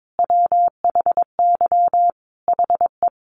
Morse code, heard as a keyed tone, about 22 wpm, 700 Hz. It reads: WHY HE
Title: Sentences from Top 700 Words 22wpm